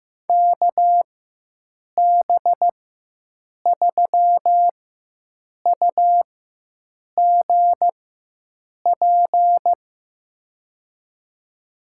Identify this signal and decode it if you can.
Morse code, heard as a keyed tone, about 15 wpm, spaced as for 7 wpm, 700 Hz. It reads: KB3UGP
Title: USA Call Signs - 6 Characters - 4x Character Spacing - 15wpm